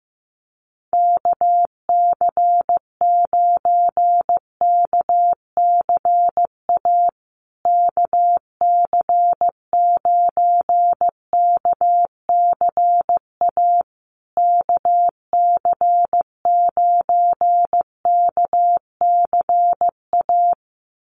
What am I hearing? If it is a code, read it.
Morse code, heard as a keyed tone, about 15 wpm, 700 Hz. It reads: KC9KCA KC9KCA KC9KCA
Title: USA Call Signs Repeated 3x 15wpm